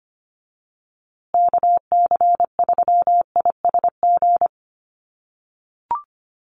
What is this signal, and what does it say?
Morse code, heard as a keyed tone, about 25 wpm, 700 Hz. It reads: KC3ISG
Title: USA Call Signs Repeated 3x 25wpm